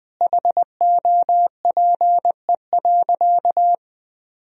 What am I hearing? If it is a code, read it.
Morse code, heard as a keyed tone, about 21 wpm, 700 Hz. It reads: HOPE.
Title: Sentences from Top 1000 Words - Review Part 1 - 20wpm